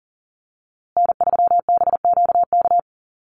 Morse code, heard as a keyed tone, about 40 wpm, 700 Hz. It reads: N3BQK
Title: USA Call Signs Repeated 2x 40wpm